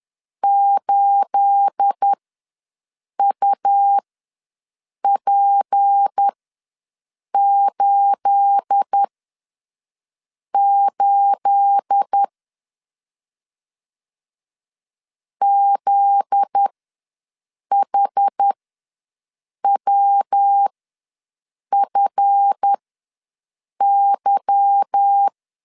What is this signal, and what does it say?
Morse code, heard as a keyed tone, about 10 wpm, spaced as for 6 wpm, 800 Hz. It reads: 8UP88 ZHWFY
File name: LEZIONI / 9 28 ws 10wpm 4124